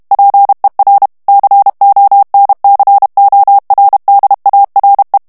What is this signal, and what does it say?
Morse code, heard as a keyed tone, about 32 wpm, 800 Hz. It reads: PER CONCORDARE
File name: ITALIEN texte clair 1 vitesse 30 mots minute